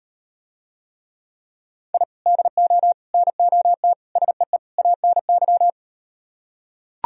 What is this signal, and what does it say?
Morse code, heard as a keyed tone, about 38 wpm, 700 Hz. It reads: I DO NOT SEE ANY